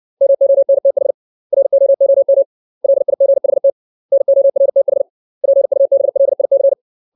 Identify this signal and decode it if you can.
Morse code, heard as a keyed tone, about 60 wpm, 550 Hz. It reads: MONTH ROOM LIGHT NORTH PUBLIC
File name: 632 words / 632 Words 60WPM